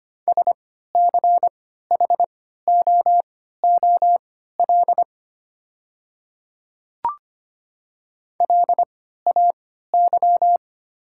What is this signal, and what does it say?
Morse code, heard as a keyed tone, about 25 wpm, spaced as for 14 wpm, 700 Hz. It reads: SCHOOL LAY
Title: 3x Character Spacing - Top 500 Words - 25wpm